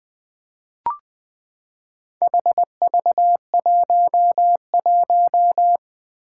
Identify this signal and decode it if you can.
Morse code, heard as a keyed tone, about 20 wpm, 700 Hz. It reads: HV11
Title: Four Challenging Letter-Number Combinations 20wpm